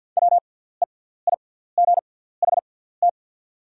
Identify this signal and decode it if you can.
Morse code, heard as a keyed tone, about 48 wpm, spaced as for 17 wpm, 690 Hz. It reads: WEIGHT
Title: 6x Character Spacing - Top 500 Words - 50wpm